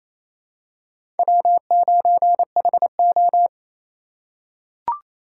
Morse code, heard as a keyed tone, about 28 wpm, 700 Hz. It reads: W9HO
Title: USA Call Signs - 4 Characters - 3x Character Spacing - 28wpm